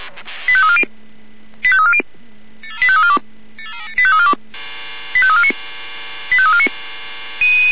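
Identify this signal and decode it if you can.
Five-tone selective call (ZVEI), heard as quick runs of five tones, tones of about 70 ms. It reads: 85340 85340 85342 85349 85349
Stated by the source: ZVEI five-tone selective call received by SDR.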